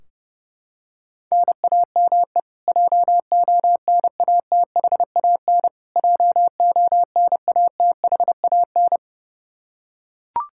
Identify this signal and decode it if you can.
Morse code, heard as a keyed tone, about 30 wpm, 700 Hz. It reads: NAME JONATHAN JONATHAN
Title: Practice QSO - Exchange 1 - Beginning - 30wpm